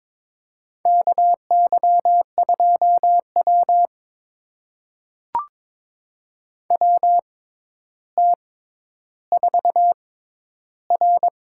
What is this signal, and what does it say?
Morse code, heard as a keyed tone, about 21 wpm, spaced as for 11 wpm, 700 Hz. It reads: KY2W WT4R
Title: USA Call Signs - 4 Characters - 6x Character Spacing - 22wpm